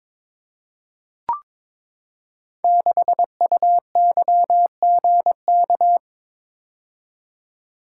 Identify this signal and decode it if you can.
Morse code, heard as a keyed tone, about 22 wpm, 700 Hz. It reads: 6UYGK
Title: Code Groups - 5 Letter:Numbers - 22wpm